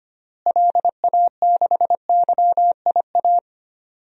Morse code, heard as a keyed tone, about 25 wpm, 700 Hz. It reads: LA6YIA
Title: DXCC-100 Call Signs - 6 Characters - Repeated 3x - 25wpm